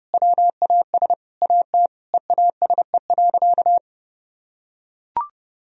Morse code, heard as a keyed tone, about 32 wpm, 700 Hz. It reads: WAS AT EASE.
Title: Sentences from Top 1000 Words - Review Part 1 - 30wpm